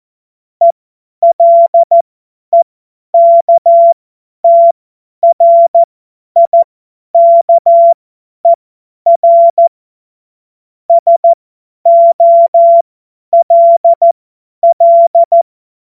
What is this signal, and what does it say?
Morse code, heard as a keyed tone, about 14 wpm, spaced as for 10 wpm, 680 Hz. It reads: ELEKTRIKER SOLL
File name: Morsekurs / Witze / Witz 13 Februar 2024 50 F70